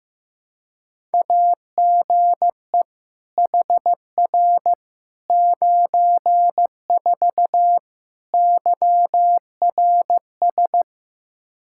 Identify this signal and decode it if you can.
Morse code, heard as a keyed tone, about 15 wpm, 700 Hz. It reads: AGE HR 94 YRS
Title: Practice QSO - Exchange 3 - More Info - 15wpm